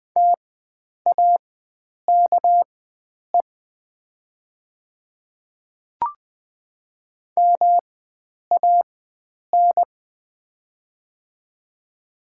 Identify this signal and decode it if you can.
Morse code, heard as a keyed tone, about 20 wpm, spaced as for 9 wpm, 700 Hz. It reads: TAKE MAN
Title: 4x Character Spacing - Top 500 Words - 20wpm